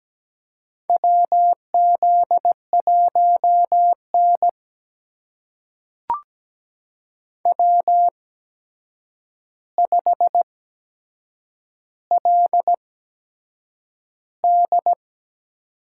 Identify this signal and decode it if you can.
Morse code, heard as a keyed tone, about 17 wpm, spaced as for 7 wpm, 700 Hz. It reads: WZ1N W5LD
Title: USA Call Signs - 4 Characters - 8x Character Spacing - 17wpm